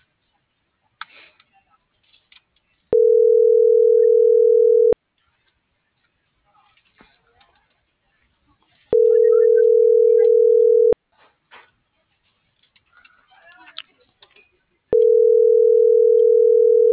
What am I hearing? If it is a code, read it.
Ringback tone (North America), 440 + 480 Hz.